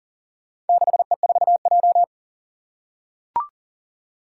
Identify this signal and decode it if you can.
Morse code, heard as a keyed tone, about 42 wpm, 700 Hz. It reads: 6E4J
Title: Four Challenging Letter-Number Combinations 40wpm